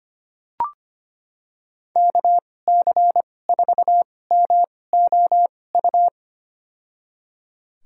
Morse code, heard as a keyed tone, about 25 wpm, spaced as for 18 wpm, 700 Hz. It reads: KC4MOU
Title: USA Call Signs - 2x Character Spacing - 25wpm